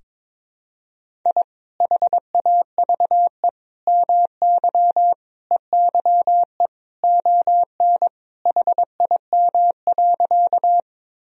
Morse code, heard as a keyed tone, about 22 wpm, 700 Hz. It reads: I HAVE MY EYE ON HIM.
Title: Sentences from Top 1000 Words - Review Part 1 - 22wpm